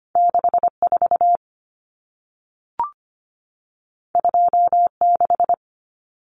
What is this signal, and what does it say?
Morse code, heard as a keyed tone, about 25 wpm, 700 Hz. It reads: 64 26
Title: Common Cut Numbers - 2-Digit 25wpm